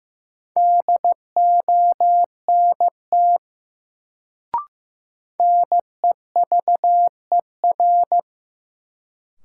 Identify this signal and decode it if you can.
Morse code, heard as a keyed tone, about 15 wpm, 700 Hz. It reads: DONT NEVER
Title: Sentences from Top 300 Words - New Word Drill - 15wpm